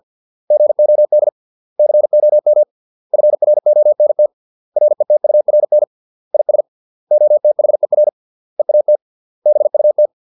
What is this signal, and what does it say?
Morse code, heard as a keyed tone, about 50 wpm, 600 Hz. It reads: GOD COM FRONT RETURN IS OTHER EAT BUT